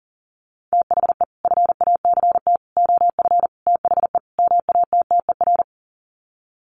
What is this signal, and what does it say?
Morse code, heard as a keyed tone, about 40 wpm, 700 Hz. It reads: THE FACT OF THE MATTER
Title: Sets of 5 Words 40wpm